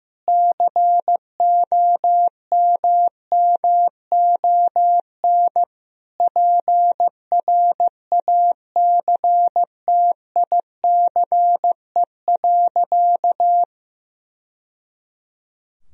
Morse code, heard as a keyed tone, about 16 wpm, 700 Hz. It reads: COMMON PRACTICE.
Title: Sentences from Top 1000 Words - Review Part 4 - 15wpm